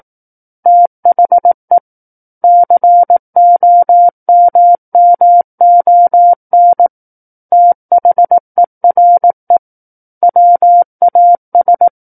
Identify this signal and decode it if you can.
Morse code, heard as a keyed tone, about 18 wpm, 700 Hz. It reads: THE COMMON THERE WAS
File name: 18 WPM